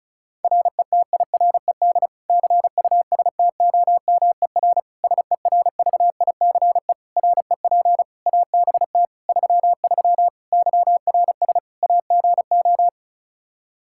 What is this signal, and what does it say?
Morse code, heard as a keyed tone, about 36 wpm, 700 Hz. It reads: RETIRED CUSTOMER SERVICE REP ABT 33 YRS AGO
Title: Practice QSO - Exchange 3 - More Info - 35wpm